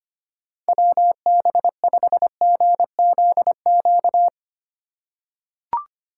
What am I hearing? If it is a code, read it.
Morse code, heard as a keyed tone, about 25 wpm, 700 Hz. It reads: WB5GZQ
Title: USA Call Signs - 6 Characters - 3x Character Spacing - 25wpm